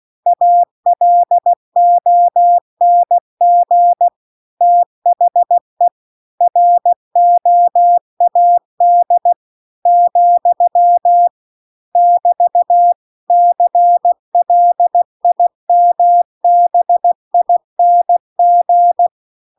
Morse code, heard as a keyed tone, about 16 wpm, 700 Hz. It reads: ALONG THE ROAD , <BT> CLIMBING